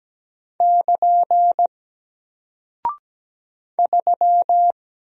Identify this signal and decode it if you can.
Morse code, heard as a keyed tone, about 17 wpm, 700 Hz. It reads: <KN> 3
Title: Single Character (Letter:Number:Punctuation:Prosign) Alt 17wpm